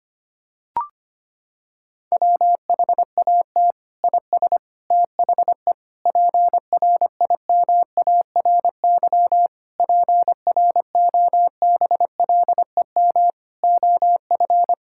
Morse code, heard as a keyed tone, about 25 wpm, 700 Hz. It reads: WHAT IS THE PRIMARY PROBLEM OF
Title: Sentences from Top 1000 Words - Review Part 5 - 25wpm